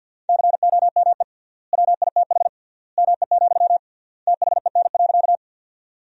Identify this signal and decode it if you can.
Morse code, heard as a keyed tone, about 47 wpm, 700 Hz. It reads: COME WITH ME, THEN.
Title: 3x Word-Spacing - Sentences from Top 200 Words - 50wpm